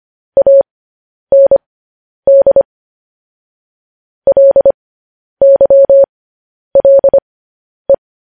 Morse code, heard as a keyed tone, about 25 wpm, spaced as for 10 wpm, 550 Hz. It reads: AND LYLE
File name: podcast / News headlines-1x-25x10